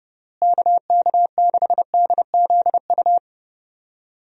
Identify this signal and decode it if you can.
Morse code, heard as a keyed tone, about 30 wpm, 700 Hz. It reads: KK6DZU
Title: USA Call Signs - 6 Characters - Repeated 3x - 30wpm